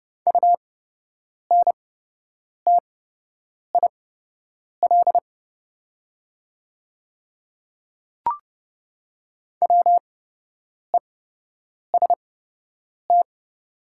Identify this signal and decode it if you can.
Morse code, heard as a keyed tone, about 30 wpm, spaced as for 8 wpm, 690 Hz. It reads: UNTIL WEST
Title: 8x Character Spacing - Top 500 Words - 30wpm